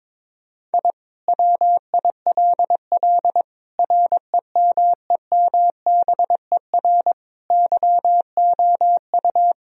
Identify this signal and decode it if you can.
Morse code, heard as a keyed tone, about 22 wpm, 700 Hz. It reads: I WILL REMEMBER YOU
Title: Sentences from Top 1000 Words - Review Part 2 - 22wpm